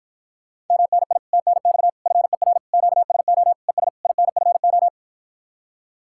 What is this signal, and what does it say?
Morse code, heard as a keyed tone, about 54 wpm, 700 Hz. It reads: MNI TNX FER QSO ES INFO